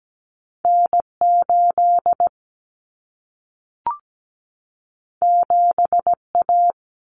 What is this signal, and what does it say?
Morse code, heard as a keyed tone, about 17 wpm, 700 Hz. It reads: N8 7A
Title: Common Cut Numbers - 2-Digit 17wpm